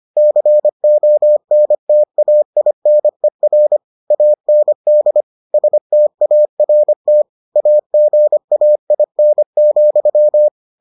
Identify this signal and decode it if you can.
Morse code, heard as a keyed tone, about 25 wpm, 600 Hz. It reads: CONTAINER AND START AGAIN,